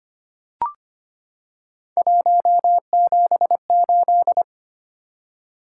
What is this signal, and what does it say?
Morse code, heard as a keyed tone, about 25 wpm, 700 Hz. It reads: 178
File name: Three Letter-Number Combinations - 25wpm